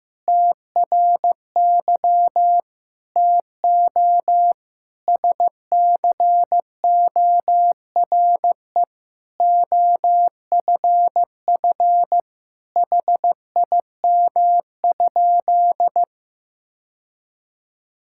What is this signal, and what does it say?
Morse code, heard as a keyed tone, about 15 wpm, 700 Hz. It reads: TRY TO SCORE OFF HIM?
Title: Sentences from Top 1600 Words 15wpm